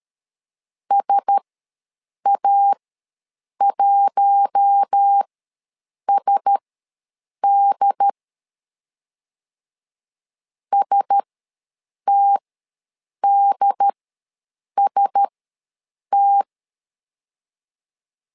Morse code, heard as a keyed tone, about 13 wpm, spaced as for 7 wpm, 800 Hz. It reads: SA1SD STDST
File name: LEZIONI / 4 Q1S ws 12wpm 1827